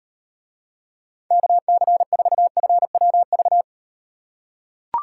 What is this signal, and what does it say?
Morse code, heard as a keyed tone, about 38 wpm, 700 Hz. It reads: KC4FWV